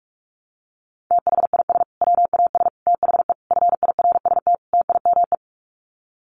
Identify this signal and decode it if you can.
Morse code, heard as a keyed tone, about 44 wpm, 700 Hz. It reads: THIS WAS THE FIRST TIME
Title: Sets of 5 Words Encore 45wpm